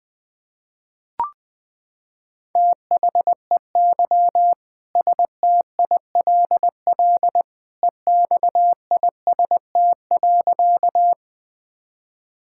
Morse code, heard as a keyed tone, about 21 wpm, 700 Hz. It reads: THEY STILL EXIST.
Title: Sentences from Top 1000 Words - Review Part 5 - 20wpm